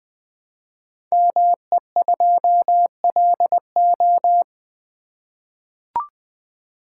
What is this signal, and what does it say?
Morse code, heard as a keyed tone, about 20 wpm, 700 Hz. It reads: ME2LO
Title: Code Groups - 5 Letter:Numbers - 20wpm